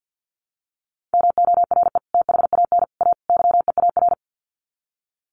Morse code, heard as a keyed tone, about 50 wpm, 700 Hz. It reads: MORE THAN A YEAR
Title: Sets of 4 Words 50wpm